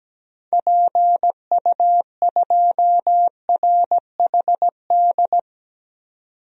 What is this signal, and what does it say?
Morse code, heard as a keyed tone, about 17 wpm, 700 Hz. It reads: PU2RHD
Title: DXCC-100 Call Signs - 6 Characters - 17wpm